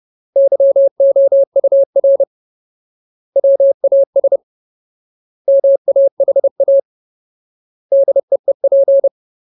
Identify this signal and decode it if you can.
Morse code, heard as a keyed tone, about 30 wpm, 550 Hz. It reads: YOUR WAS MAHA DEEP
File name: Ultimate Word List / 30WPM Ultimate